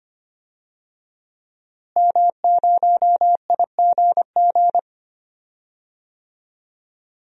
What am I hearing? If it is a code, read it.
Morse code, heard as a keyed tone, about 25 wpm, 700 Hz. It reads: M0IGG
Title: DXCC-100 Call Signs - 5 Characters - Repeated 2x - 25wpm